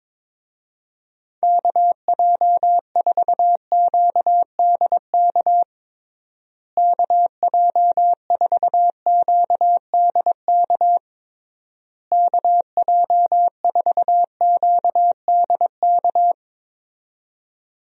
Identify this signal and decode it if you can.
Morse code, heard as a keyed tone, about 21 wpm, 700 Hz. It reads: KJ4QDK KJ4QDK KJ4QDK
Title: USA Call Signs - 6 Characters - Repeated 3x - 22wpm